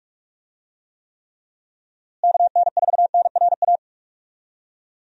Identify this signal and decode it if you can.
Morse code, heard as a keyed tone, about 46 wpm, 700 Hz. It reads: KN4NRA
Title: USA Call Signs - 6 Characters - Repeated 3x - 45wpm